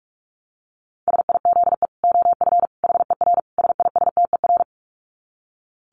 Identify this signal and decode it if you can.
Morse code, heard as a keyed tone, about 44 wpm, 700 Hz. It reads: SIZE OF HER SISTER
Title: Dit Challenge 45wpm